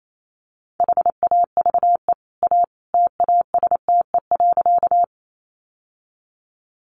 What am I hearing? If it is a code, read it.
Morse code, heard as a keyed tone, about 28 wpm, 700 Hz. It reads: HAVE A TASTE.
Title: Sentences from Top 1100 Words 28wpm